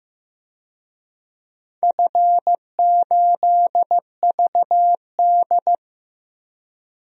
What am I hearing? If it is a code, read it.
Morse code, heard as a keyed tone, about 15 wpm, 700 Hz. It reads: F8VD
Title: DXCC-100 Call Signs - 4 Characters - 15wpm